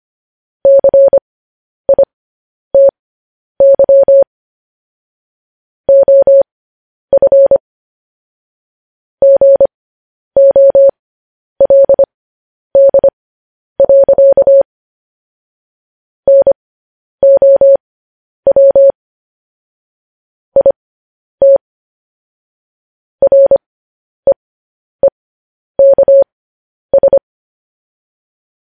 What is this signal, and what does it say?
Morse code, heard as a keyed tone, about 25 wpm, spaced as for 10 wpm, 550 Hz. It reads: CITY OF GOLD. NOW IT REEKS
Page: News Headlines in Morse Code at 10 WPM